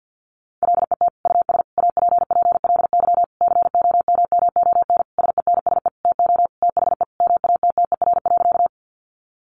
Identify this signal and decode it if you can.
Morse code, heard as a keyed tone, about 50 wpm, 700 Hz. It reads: LET US APPLY COMMON SENSE TO THE MATTER.
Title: Sentences from Top 1200 Words 50wpm